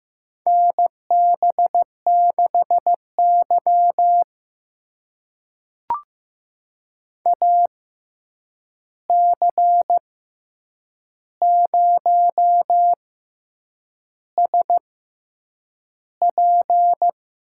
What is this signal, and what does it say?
Morse code, heard as a keyed tone, about 15 wpm, spaced as for 7 wpm, 700 Hz. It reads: NB6Y AC0SP
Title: USA Call Signs - 6x Character Spacing - 15wpm